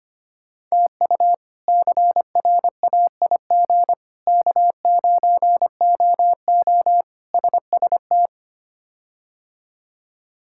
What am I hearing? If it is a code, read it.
Morse code, heard as a keyed tone, about 25 wpm, 700 Hz. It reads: TU CRAIG K9OO SST
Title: K1USN SST QSO Practice - 25wpm